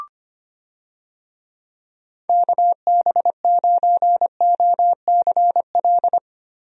Morse code, heard as a keyed tone, about 25 wpm, 700 Hz. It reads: KB9OCL
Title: USA Call Signs - 6 Characters - Repeated 3x - 25wpm